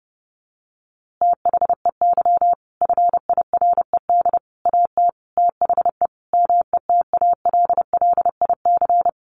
Morse code, heard as a keyed tone, about 30 wpm, 700 Hz. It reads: THEY FIRED AT THE METALLIC
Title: A Sound Of Thunder 30wpm